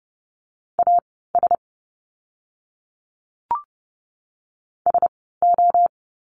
Morse code, heard as a keyed tone, about 30 wpm, spaced as for 17 wpm, 700 Hz. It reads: AS SO
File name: course / 30:10wpm / 115 Lesson 19f - Letter w - Words 30wpm